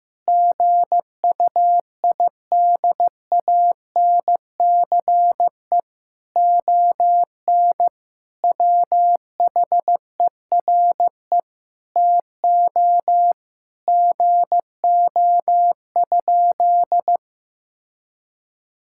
Morse code, heard as a keyed tone, about 15 wpm, 700 Hz. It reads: GUIDANCE ON WHERE TO GO?
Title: Sentences from Top 1600 Words 15wpm